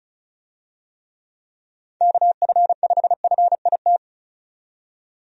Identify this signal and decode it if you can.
Morse code, heard as a keyed tone, about 34 wpm, 700 Hz. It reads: KF5FIT
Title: USA Call Signs - 6 Characters - 35wpm